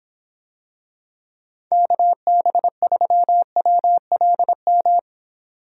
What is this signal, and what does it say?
Morse code, heard as a keyed tone, about 26 wpm, 700 Hz. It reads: KB3WLM